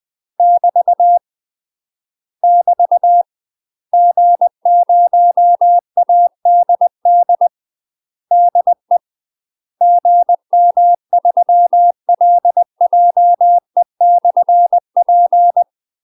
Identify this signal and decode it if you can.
Morse code, heard as a keyed tone, about 20 wpm, 700 Hz. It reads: <BT> <BT> G0ADD DE GM3LJE/P
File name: UK MASTER QSOs TRAINING / 20WPM / UK MASTER QSOs 0062